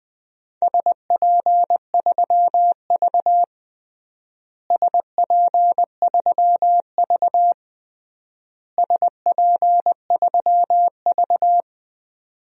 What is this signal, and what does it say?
Morse code, heard as a keyed tone, about 20 wpm, 700 Hz. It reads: SP3V SP3V SP3V
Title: DXCC-100 Call Signs - 4 Characters - Repeated 3x - 20wpm